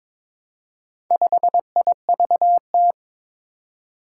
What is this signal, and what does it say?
Morse code, heard as a keyed tone, about 22 wpm, 700 Hz. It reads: 5IVT
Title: Four Challenging Letter-Number Combinations 22wpm